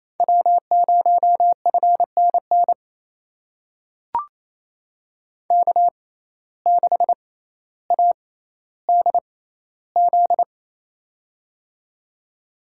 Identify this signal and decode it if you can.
Morse code, heard as a keyed tone, about 28 wpm, spaced as for 14 wpm, 700 Hz. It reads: W0FNN K6ADZ
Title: USA Call Signs - 5 Characters - 6x Character Spacing - 28wpm